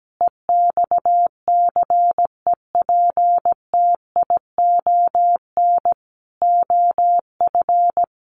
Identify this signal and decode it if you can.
Morse code, heard as a keyed tone, about 17 wpm, 700 Hz. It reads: EXCEPTION OF
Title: Sets of 4 Words 17wpm